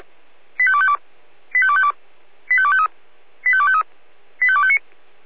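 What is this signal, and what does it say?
Five-tone selective call (ZVEI), heard as quick runs of five tones, tones of about 70 ms. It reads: 85252 85252 85253 85253 85259
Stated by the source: ZVEI five-tone selective call received with an SDR.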